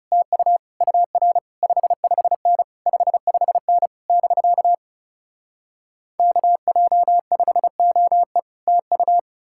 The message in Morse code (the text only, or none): TU UR 55N 55N <BK> KJ5OE TU